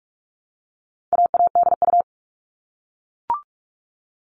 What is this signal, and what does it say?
Morse code, heard as a keyed tone, about 44 wpm, 700 Hz. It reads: AADU
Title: All Cut Numbers - 4-Digit 45wpm